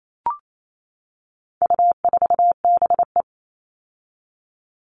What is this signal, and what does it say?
Morse code, heard as a keyed tone, about 28 wpm, 700 Hz. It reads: U4BE
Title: All Cut Numbers - 4-Digit 28wpm